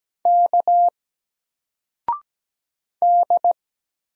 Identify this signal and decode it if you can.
Morse code, heard as a keyed tone, about 17 wpm, 700 Hz. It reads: K D
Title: Single Letter-Number 17wpm